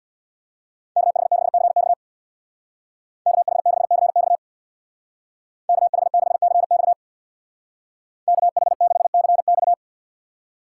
Morse code, heard as a keyed tone, about 60 wpm, 690 Hz. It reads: KH6CX KH6CX KH6CX KH6CX
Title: Speed Racing - USA 2x2 Call Signs 50wpm